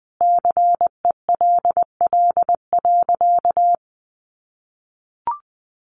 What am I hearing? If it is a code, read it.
Morse code, heard as a keyed tone, about 20 wpm, 700 Hz. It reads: CELL.
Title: Sentences from Top 1100 Words 20wpm